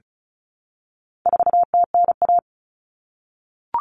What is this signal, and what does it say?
Morse code, heard as a keyed tone, about 34 wpm, 700 Hz. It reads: 4TNA
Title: Common Cut Numbers - 4-Digit 35wpm